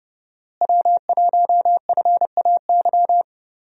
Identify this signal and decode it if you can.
Morse code, heard as a keyed tone, about 30 wpm, 700 Hz. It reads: W1FAY
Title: USA Call Signs - 5 Characters - Repeated 2x - 30wpm